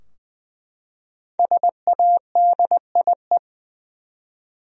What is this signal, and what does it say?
Morse code, heard as a keyed tone, about 20 wpm, 700 Hz. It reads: SADIE
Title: USA Common Names - Max 5 Letters - 20wpm